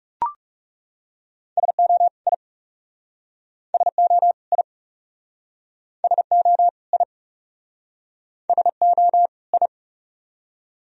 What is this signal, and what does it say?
Morse code, heard as a keyed tone, about 40 wpm, 700 Hz. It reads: SO I SO I SO I SO I